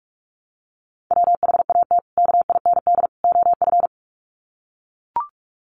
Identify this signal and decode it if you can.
Morse code, heard as a keyed tone, about 44 wpm, 700 Hz. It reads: WHAT KIND OF
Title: Sets of 3 Words 45wpm